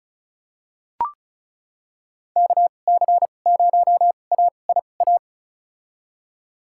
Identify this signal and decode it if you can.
Morse code, heard as a keyed tone, about 36 wpm, spaced as for 25 wpm, 700 Hz. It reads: KC0AIA
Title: USA Call Signs - 6 Characters - 2x Character Spacing - 35wpm